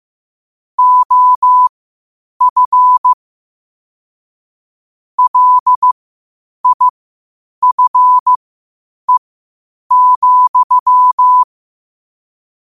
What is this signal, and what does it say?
Morse code, heard as a keyed tone, about 15 wpm, spaced as for 8 wpm, 1000 Hz. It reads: OF LIFE,
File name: qotdfiles / 8wpm